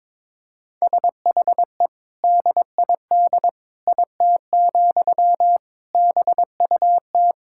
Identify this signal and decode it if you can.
Morse code, heard as a keyed tone, about 22 wpm, 700 Hz. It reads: SHE DID IT, BUT
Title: Sentences from Top 1000 Words - Review Part 6 - 22wpm